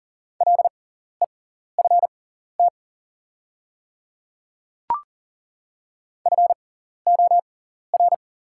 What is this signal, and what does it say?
Morse code, heard as a keyed tone, about 41 wpm, spaced as for 14 wpm, 690 Hz. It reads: LEFT FOR
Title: 6x Character Spacing - Top 300 Words - 40wpm